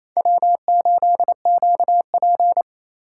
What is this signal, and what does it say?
Morse code, heard as a keyed tone, about 28 wpm, 700 Hz. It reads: W8QP
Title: USA Call Signs - 4 Characters - 8x Character Spacing - 28wpm